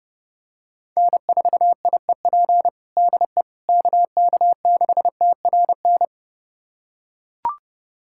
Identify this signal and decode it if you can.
Morse code, heard as a keyed tone, about 30 wpm, 700 Hz. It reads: N4IEP DE KK6TRN
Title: USA Callsign DE Callsign 30wpm